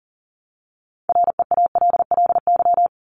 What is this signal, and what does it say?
Morse code, heard as a keyed tone, about 40 wpm, 700 Hz. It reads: REALLY
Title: LY Suffix - Top 100 Words with Base Words 40wpm